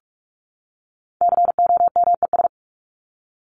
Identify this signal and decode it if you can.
Morse code, heard as a keyed tone, about 44 wpm, 700 Hz. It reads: COMES
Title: ES:S Suffix - Top 100 Words with Base Words 45wpm